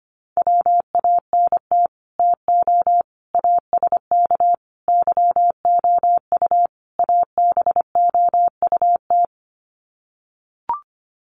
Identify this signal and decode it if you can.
Morse code, heard as a keyed tone, about 25 wpm, 700 Hz. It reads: WANT TO ASK YOU ABOUT
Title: Sets of 5 Words Encore2 25wpm